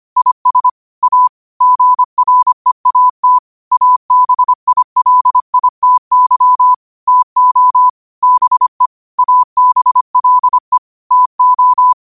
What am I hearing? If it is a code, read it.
Morse code, heard as a keyed tone, about 25 wpm, 1000 Hz. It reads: IS A GREAT ABILITY TO BE ABLE TO
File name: qotdfiles / 25wpm